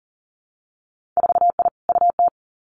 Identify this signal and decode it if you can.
Morse code, heard as a keyed tone, about 42 wpm, 700 Hz. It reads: 4I UT
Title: Winter Field Day Exchanges 40wpm